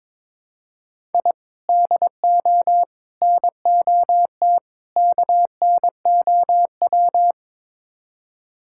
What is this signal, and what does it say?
Morse code, heard as a keyed tone, about 22 wpm, 700 Hz. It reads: I DO NOT KNOW
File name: practice / 22wpm / Build A Phrase - 5-Word Phrases - 22wpm